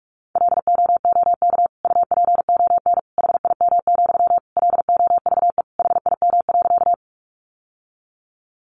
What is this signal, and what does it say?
Morse code, heard as a keyed tone, about 44 wpm, 700 Hz. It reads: LOOK UPON HIM, LOVE HIM.
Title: Sentences from Top 1100 Words 45wpm